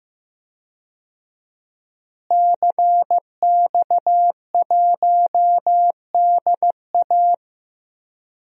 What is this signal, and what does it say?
Morse code, heard as a keyed tone, about 15 wpm, 700 Hz. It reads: CX1DA
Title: DXCC-100 Call Signs - 5 Characters - Repeated 3x - 15wpm